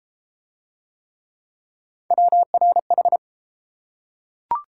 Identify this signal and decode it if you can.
Morse code, heard as a keyed tone, about 27 wpm, 700 Hz. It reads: WRH
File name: Speed-Racing - Three Letter-Number Combinations - 22wpm